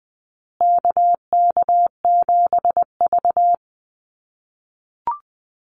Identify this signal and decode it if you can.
Morse code, heard as a keyed tone, about 20 wpm, 700 Hz. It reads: KK7V
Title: USA Call Signs Repeated 2x 20wpm